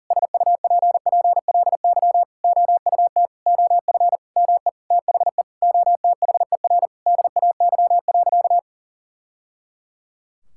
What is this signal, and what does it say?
Morse code, heard as a keyed tone, about 40 wpm, 700 Hz. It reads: SUPPLY OUT OF ME THE OTHER DAY.
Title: Sentences from Top 1000 Words - Review Part 5 - 40wpm